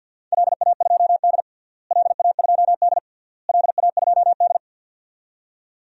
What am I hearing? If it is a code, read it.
Morse code, heard as a keyed tone, about 50 wpm, 700 Hz. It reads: LA2D LA2D LA2D
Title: DXCC-100 Call Signs - 4 Characters - Repeated 3x - 50wpm